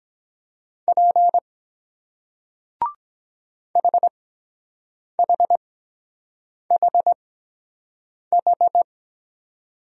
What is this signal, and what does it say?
Morse code, heard as a keyed tone, about 23 wpm, spaced as for 7 wpm, 700 Hz. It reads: P HHHH